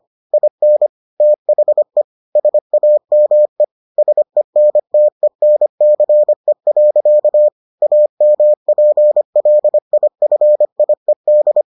Morse code, heard as a keyed tone, about 25 wpm, 600 Hz. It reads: IN THE SAME SENTENCE. AMPLIFIED